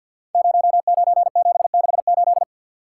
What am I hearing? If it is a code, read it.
Morse code, heard as a keyed tone, about 49 wpm, 700 Hz. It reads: 09768
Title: Code Groups - 5 Letter:Numbers - 50wpm